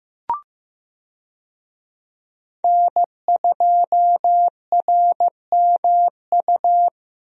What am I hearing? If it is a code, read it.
Morse code, heard as a keyed tone, about 15 wpm, 700 Hz. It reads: N2RMU
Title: USA Call Signs - 5 Characters - Repeated 2x - 15wpm